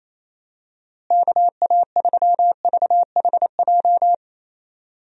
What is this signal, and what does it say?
Morse code, heard as a keyed tone, about 28 wpm, 700 Hz. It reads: KA3VHJ
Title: USA Call Signs - 6 Characters - Repeated 2x - 28wpm